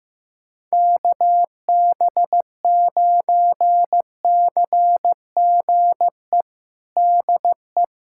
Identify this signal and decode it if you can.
Morse code, heard as a keyed tone, about 15 wpm, 700 Hz. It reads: KB9CGE DE
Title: USA Callsign DE Callsign - 6 Characters - 15wpm